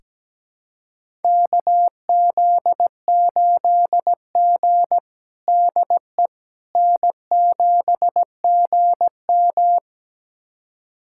Morse code, heard as a keyed tone, about 17 wpm, 700 Hz. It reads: KZ8G DE N7GM
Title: USA Callsign DE Callsign - 4 Characters - 17wpm